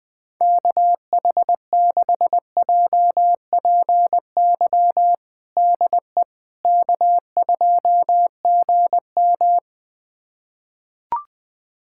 Morse code, heard as a keyed tone, about 21 wpm, 700 Hz. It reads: KH6JPY DE K2GM
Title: USA Callsign DE Callsign 20wpm